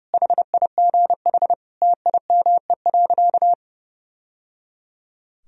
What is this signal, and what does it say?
Morse code, heard as a keyed tone, about 29 wpm, 700 Hz. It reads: HIGH TIME.
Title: Sentences from Top 1000 Words - Review Part 1 - 30wpm